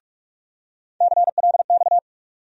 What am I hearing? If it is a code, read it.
Morse code, heard as a keyed tone, about 42 wpm, 700 Hz. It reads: CLX
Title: USA Call Sign Postfixes - 3 Letters 45wpm